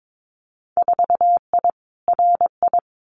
Morse code, heard as a keyed tone, about 22 wpm, 700 Hz. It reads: 4I RI
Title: Winter Field Day Exchanges Encore 22wpm